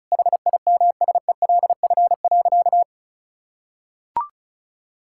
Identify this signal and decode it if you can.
Morse code, heard as a keyed tone, about 36 wpm, 700 Hz. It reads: HIMSELF.
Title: Sentences from Top 1600 Words 35wpm